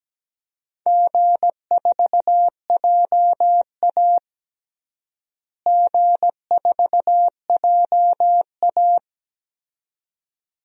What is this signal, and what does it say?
Morse code, heard as a keyed tone, about 17 wpm, 700 Hz. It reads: G4JA G4JA
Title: DXCC-100 Call Signs - 4 Characters - Repeated 2x - 17wpm